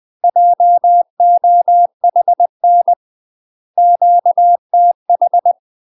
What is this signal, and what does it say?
Morse code, heard as a keyed tone, about 20 wpm, 700 Hz. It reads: JOHN QTH
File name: UK MASTER QSOs TRAINING / 20WPM / UK MASTER QSOs 0040